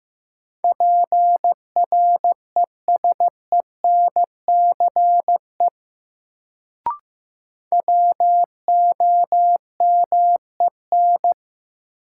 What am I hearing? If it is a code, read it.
Morse code, heard as a keyed tone, about 15 wpm, 700 Hz. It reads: PRESENCE WOMEN
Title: Sets of 3 Words - New Word Drill - 15wpm